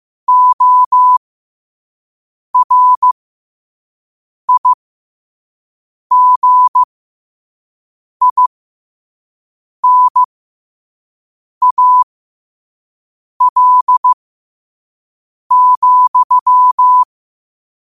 Morse code, heard as a keyed tone, about 15 wpm, spaced as for 5 wpm, 1000 Hz. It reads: ORIGINAL,